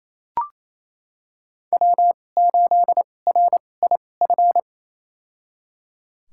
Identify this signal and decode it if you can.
Morse code, heard as a keyed tone, about 28 wpm, spaced as for 20 wpm, 700 Hz. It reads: W8RIF
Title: USA Call Signs - 2x Character Spacing - 28wpm